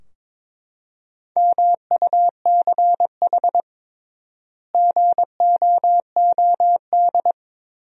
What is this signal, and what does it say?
Morse code, heard as a keyed tone, about 22 wpm, 700 Hz. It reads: MUCH GOOD
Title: 3x Word-Spacing - Sentences from Top 200 Words - 22wpm